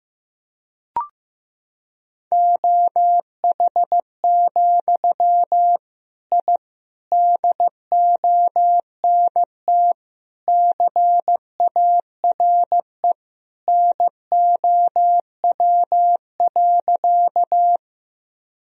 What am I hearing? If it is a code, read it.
Morse code, heard as a keyed tone, about 15 wpm, 700 Hz. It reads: OH, I DONT CARE NOW.
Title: Sentences from Top 1000 Words - Review Part 1 - 15wpm